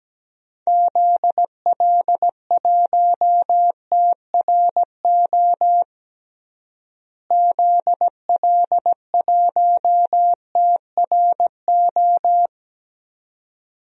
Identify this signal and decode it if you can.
Morse code, heard as a keyed tone, about 17 wpm, 700 Hz. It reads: ZL1TRO ZL1TRO
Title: DXCC-100 Call Signs - 6 Characters - Repeated 2x - 17wpm